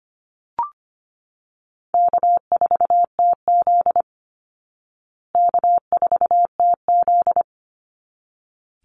Morse code, heard as a keyed tone, about 25 wpm, 700 Hz. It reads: K4TZ K4TZ
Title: Easy Call Signs Repeated 2x - 25wpm